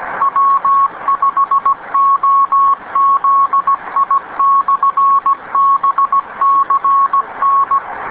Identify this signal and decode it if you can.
Morse code, heard as a keyed tone, about 17 wpm, 1100 Hz. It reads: W5OZI/BCN